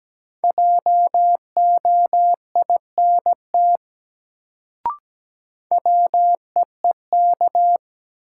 Morse code, heard as a keyed tone, about 17 wpm, 700 Hz. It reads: JOINT WEEK
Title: Sets of 5 Words Encore - New Word Drill - 17wpm